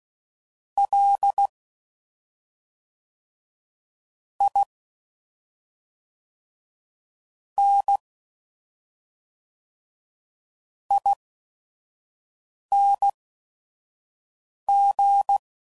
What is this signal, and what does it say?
Morse code, heard as a keyed tone, about 16 wpm, spaced as for 5 wpm, 800 Hz. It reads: L I N ING
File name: Suffixes / Special ING's / SlowToFaster ING's